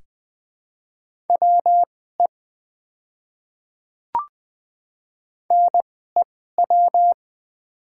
Morse code, heard as a keyed tone, about 20 wpm, spaced as for 14 wpm, 700 Hz. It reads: WE NEW
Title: 2x Character Spacing - Top 200 Words - 20wpm